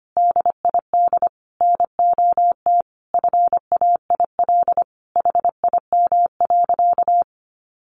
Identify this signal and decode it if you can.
Morse code, heard as a keyed tone, about 25 wpm, 700 Hz. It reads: DID NOT FAIL HIM.
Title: Sentences from Top 1100 Words 25wpm